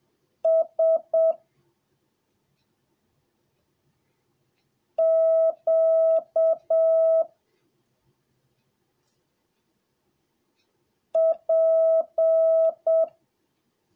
Morse code, heard as a keyed tone, about 7 wpm, 650 Hz. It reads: S Q P